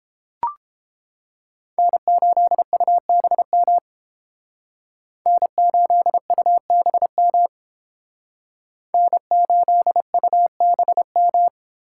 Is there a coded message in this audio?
Yes: Morse code from 1.8 to 11.5 s, heard as a keyed tone, about 30 wpm, 700 Hz. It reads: N8UBM N8UBM N8UBM